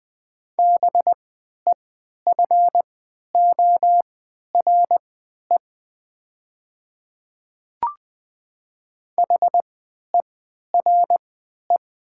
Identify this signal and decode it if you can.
Morse code, heard as a keyed tone, about 20 wpm, spaced as for 11 wpm, 690 Hz. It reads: BEFORE HERE
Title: 3x Character Spacing - Top 200 Words - 20wpm